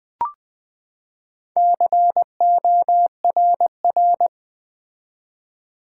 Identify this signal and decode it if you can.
Morse code, heard as a keyed tone, about 20 wpm, 700 Hz. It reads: CORR
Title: Build A Word - 4 Letters At A Time - 11-Letter Words - 20wpm